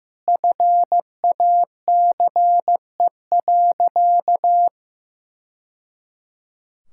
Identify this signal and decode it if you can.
Morse code, heard as a keyed tone, about 15 wpm, 700 Hz. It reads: FACE.
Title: Sentences from Top 1000 Words - Review Part 5 - 15wpm